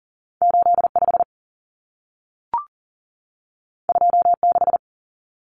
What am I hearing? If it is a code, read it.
Morse code, heard as a keyed tone, about 40 wpm, 700 Hz. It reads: 85 26
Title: Numbers - 2-Digit 40wpm